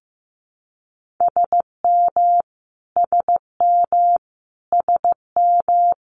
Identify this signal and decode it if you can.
Morse code, heard as a keyed tone, about 15 wpm, 700 Hz. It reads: SM SM SM
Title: Multi-letter Phonemes Repeated 3x 15wpm